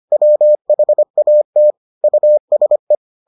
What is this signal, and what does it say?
Morse code, heard as a keyed tone, about 25 wpm, 600 Hz. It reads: WHAT USE